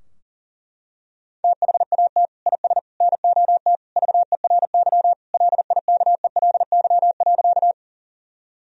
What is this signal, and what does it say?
Morse code, heard as a keyed tone, about 40 wpm, 700 Hz. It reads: THAT IS NOT VERY LIKELY.
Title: Sentences from Top 1000 Words - Review Part 4 - 40wpm